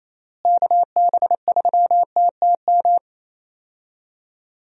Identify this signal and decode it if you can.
Morse code, heard as a keyed tone, about 28 wpm, 700 Hz. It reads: KB3TTM
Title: USA Call Signs - 6 Characters - Repeated 3x - 28wpm